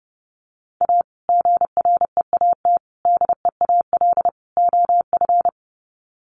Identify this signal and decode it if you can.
Morse code, heard as a keyed tone, about 30 wpm, 700 Hz. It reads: A GREAT DEAL OF
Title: Sets of 4 Words 30wpm